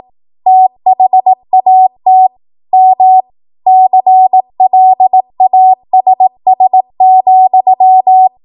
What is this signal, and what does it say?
Morse code, heard as a keyed tone, about 18 wpm, 750 Hz. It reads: THAT M CLASS,
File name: Morse / 18WPM